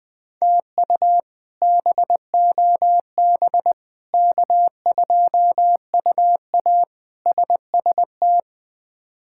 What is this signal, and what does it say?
Morse code, heard as a keyed tone, about 20 wpm, 700 Hz. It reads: TU BOB K2UA SST
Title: K1USN SST QSO Practice - 20wpm